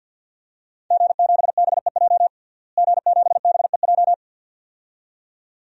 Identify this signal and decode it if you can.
Morse code, heard as a keyed tone, about 48 wpm, 700 Hz. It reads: G7BEJ G7BEJ
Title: DXCC-100 Call Signs - 5 Characters - Repeated 2x - 50wpm